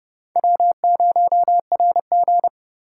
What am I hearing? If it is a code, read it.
Morse code, heard as a keyed tone, about 28 wpm, 700 Hz. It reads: W0RG